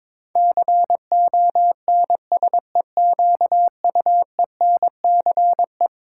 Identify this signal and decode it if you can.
Morse code, heard as a keyed tone, about 22 wpm, 700 Hz. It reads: CONSEQUENCE
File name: practice / 22wpm / Eleven Letter Words - 22wpm